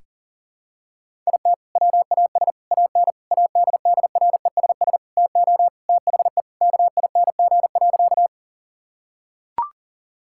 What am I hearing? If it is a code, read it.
Morse code, heard as a keyed tone, about 40 wpm, 700 Hz. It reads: IT WAS AN ADDRESS TO THE KING.
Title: Sentences from Top 1000 Words - Review Part 6 - 40wpm